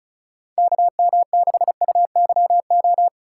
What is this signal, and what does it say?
Morse code, heard as a keyed tone, about 36 wpm, 700 Hz. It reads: KM6UYO
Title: USA Call Signs - 6 Characters - 2x Character Spacing - 35wpm